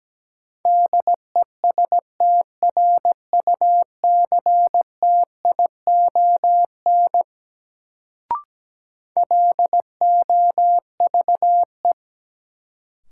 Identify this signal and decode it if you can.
Morse code, heard as a keyed tone, about 17 wpm, 700 Hz. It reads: DESTRUCTION LOVE
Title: Sets of 5 Words Encore - New Word Drill - 17wpm